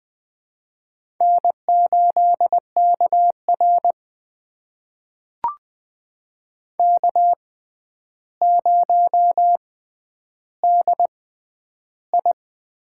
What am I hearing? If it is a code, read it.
Morse code, heard as a keyed tone, about 20 wpm, spaced as for 10 wpm, 700 Hz. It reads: N8KR K0DI
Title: USA Call Signs - 4 Characters - 6x Character Spacing - 20wpm